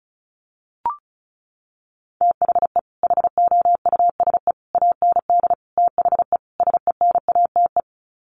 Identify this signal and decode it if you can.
Morse code, heard as a keyed tone, about 34 wpm, 700 Hz. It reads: THE HOUSE AND THE SENATE
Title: Sets of 5 Words Encore2 35wpm